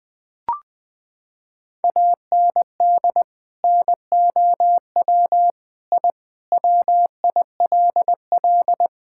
Morse code, heard as a keyed tone, about 20 wpm, 700 Hz. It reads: AND NOW I WILL
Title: Sentences from Top 1000 Words - Review Part 6 - 20wpm